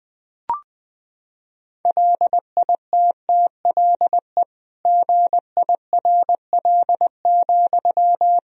Morse code, heard as a keyed tone, about 20 wpm, 700 Hz. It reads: LITTLE GIRL,
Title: Sentences from Top 400 Words and Abbreviations 20wpm